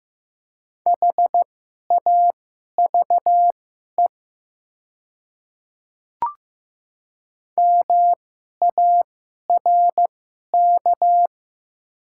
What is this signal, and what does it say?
Morse code, heard as a keyed tone, about 15 wpm, spaced as for 11 wpm, 700 Hz. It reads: HAVE MARK
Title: 2x Character Spacing - Top 500 Words - 15wpm